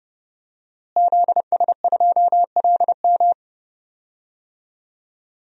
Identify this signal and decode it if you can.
Morse code, heard as a keyed tone, about 29 wpm, 700 Hz. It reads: ZS2LM
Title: DXCC-100 Call Signs - 5 Characters - Repeated 3x - 30wpm